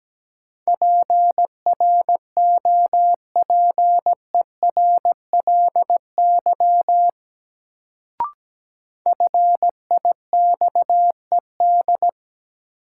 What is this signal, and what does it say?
Morse code, heard as a keyed tone, about 17 wpm, 700 Hz. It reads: PROPERLY FIXED
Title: Sentences from Top 1100 Words - New Word Drill - 17wpm